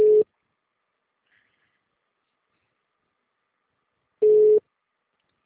Ringback tone (Europe, 425 Hz).